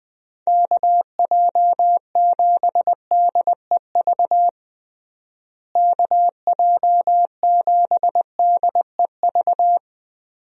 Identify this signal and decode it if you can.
Morse code, heard as a keyed tone, about 20 wpm, 700 Hz. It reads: KJ7DEV KJ7DEV
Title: USA Call Signs - 6 Characters - Repeated 2x - 20wpm